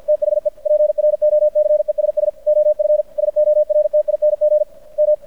Morse code, heard as a keyed tone, about 52 wpm, 600 Hz. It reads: THE PROCESS OF SORTING M
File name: archived / CRSnet